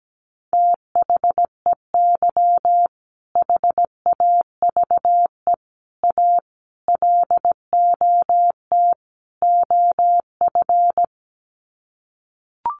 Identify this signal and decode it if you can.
Morse code, heard as a keyed tone, about 17 wpm, 700 Hz. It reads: THEY HAVE A LOT OF
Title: Sets of 5 Words Encore2 17wpm